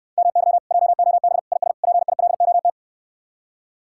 Morse code, heard as a keyed tone, about 69 wpm, 700 Hz. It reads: MY JOB IS PILOT